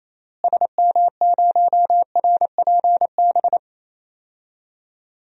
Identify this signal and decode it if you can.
Morse code, heard as a keyed tone, about 28 wpm, 700 Hz. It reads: SM0RPB
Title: DXCC-100 Call Signs - 6 Characters - Repeated 3x - 28wpm